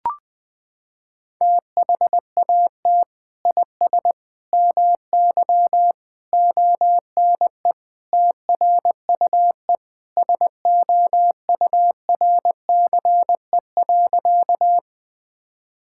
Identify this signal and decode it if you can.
Morse code, heard as a keyed tone, about 21 wpm, 700 Hz. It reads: THAT IS MY ONE TRUE SOURCE.
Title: Sentences from Top 1000 Words - Review Part 5 - 20wpm